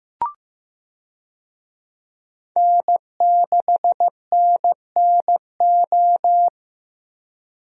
Morse code, heard as a keyed tone, about 15 wpm, 700 Hz. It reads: N6NNO
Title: USA Call Signs - 5 Characters - Repeated 3x - 15wpm